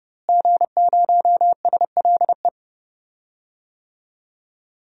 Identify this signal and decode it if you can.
Morse code, heard as a keyed tone, about 30 wpm, 700 Hz. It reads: G0SLE
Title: DXCC-100 Call Signs - 5 Characters - Repeated 3x - 30wpm